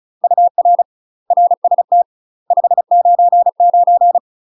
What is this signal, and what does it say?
Morse code, heard as a keyed tone, about 36 wpm, 700 Hz. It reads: UR RST 599
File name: UK MASTER QSOs TRAINING / 35WPM / UK MASTER QSOs 0023